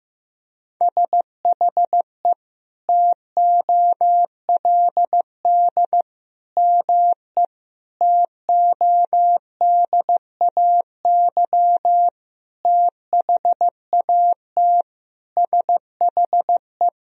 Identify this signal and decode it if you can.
Morse code, heard as a keyed tone, about 15 wpm, 700 Hz. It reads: SHE TOLD ME TODAY THAT SHE
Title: Sentences from Top 1000 Words - Review Part 4 - 15wpm